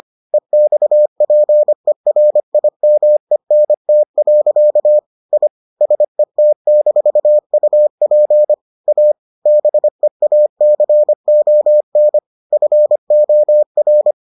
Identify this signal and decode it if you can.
Morse code, heard as a keyed tone, about 25 wpm, 600 Hz. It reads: EXPERIMENT. I SET-UP A BEACON FOR